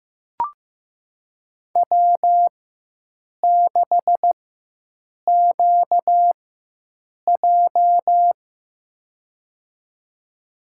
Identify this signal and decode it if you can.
Morse code, heard as a keyed tone, about 15 wpm, spaced as for 7 wpm, 700 Hz. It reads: W6QJ
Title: USA Call Signs - 4x Character Spacing - 15wpm